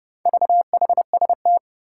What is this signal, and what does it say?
Morse code, heard as a keyed tone, about 30 wpm, 700 Hz. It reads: VHST
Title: Four Challenging Letter-Number Combinations 30wpm